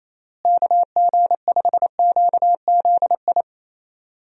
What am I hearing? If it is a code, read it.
Morse code, heard as a keyed tone, about 27 wpm, 700 Hz. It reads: KG5QZI
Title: USA Call Signs - 6 Characters - Repeated 3x - 28wpm